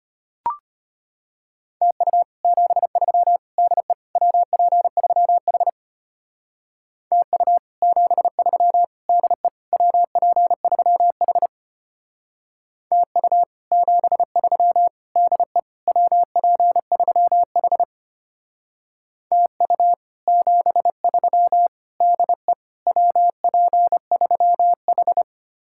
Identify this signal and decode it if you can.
Morse code, heard as a keyed tone, about 33 wpm, 700 Hz. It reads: TU 73 DE WP3H TU 73 DE WP3H TU 73 DE WP3H TU 73 DE WP3H